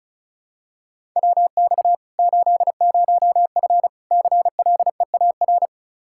Morse code, heard as a keyed tone, about 36 wpm, 700 Hz. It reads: WX 80F CLEAR
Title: Practice QSO - Exchange 2 - Info - 35wpm